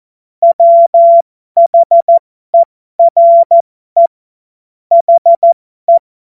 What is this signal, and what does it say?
Morse code, heard as a keyed tone, about 14 wpm, spaced as for 12 wpm, 680 Hz. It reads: WHERE HE